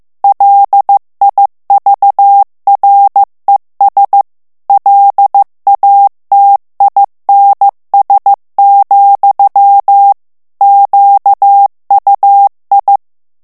Morse code, heard as a keyed tone, about 15 wpm, 800 Hz. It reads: LIVRES LATINS, QUI
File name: morse mp3 / FRANCAIS TEXTE CLAIR 2 VITESSE 14 MOTS MINUTES